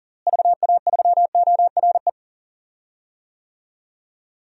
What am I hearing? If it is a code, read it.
Morse code, heard as a keyed tone, about 40 wpm, 700 Hz. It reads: VA3ORE
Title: DXCC-100 Call Signs - 6 Characters - Repeated 2x - 40wpm